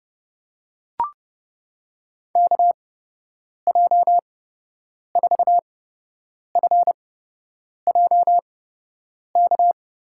Morse code, heard as a keyed tone, about 30 wpm, spaced as for 8 wpm, 700 Hz. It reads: KJ4FJK
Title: USA Call Signs - 8x Character Spacing - 30wpm